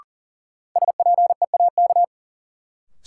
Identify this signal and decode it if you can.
Morse code, heard as a keyed tone, about 40 wpm, 700 Hz. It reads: SPEAK
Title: Sentences from Top 1100 Words - New Word Drill Alt - 40wpm